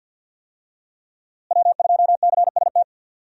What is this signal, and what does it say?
Morse code, heard as a keyed tone, about 53 wpm, 700 Hz. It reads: W2CST
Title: USA Call Signs - 5 Characters - Repeated 2x - 50wpm